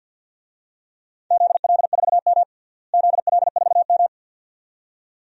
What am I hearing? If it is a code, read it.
Morse code, heard as a keyed tone, about 51 wpm, 700 Hz. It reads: ZL4M ZL4M
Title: DXCC-100 Call Signs - 4 Characters - Repeated 2x - 50wpm